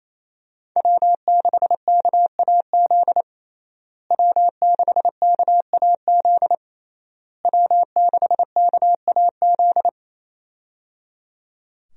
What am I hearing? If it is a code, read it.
Morse code, heard as a keyed tone, about 28 wpm, 700 Hz. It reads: W6KAZ W6KAZ W6KAZ
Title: USA Call Signs - 5 Characters - Repeated 3x - 28wpm